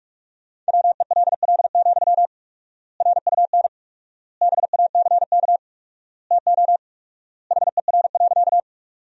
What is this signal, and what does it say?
Morse code, heard as a keyed tone, about 46 wpm, 700 Hz. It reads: WELL, RUN BACK TO HER.
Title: 4x Word-Spacing - Sentences from Top 300 Words - 45wpm